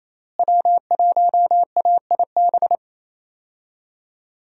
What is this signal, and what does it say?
Morse code, heard as a keyed tone, about 28 wpm, 700 Hz. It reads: W1AIB
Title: USA Call Signs - 5 Characters - Repeated 2x - 28wpm